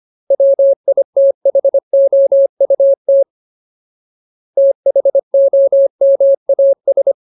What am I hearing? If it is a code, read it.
Morse code, heard as a keyed tone, about 25 wpm, 550 Hz. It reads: WITHOUT THOMAS
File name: Ultimate Word List / 25WPM Ultimate